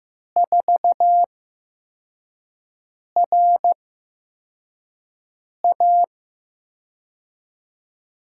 Morse code, heard as a keyed tone, about 15 wpm, spaced as for 4 wpm, 700 Hz. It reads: 4RA
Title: USA Call Signs - 5 Characters - 8x Character Spacing - 15wpm